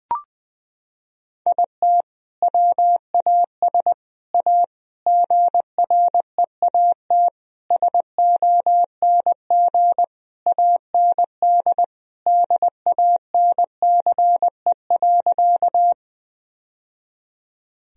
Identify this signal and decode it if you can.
Morse code, heard as a keyed tone, about 20 wpm, 700 Hz. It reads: IT WAS A GREAT SONG AND DANCE.
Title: Sentences from Top 1000 Words - Review Part 4 - 20wpm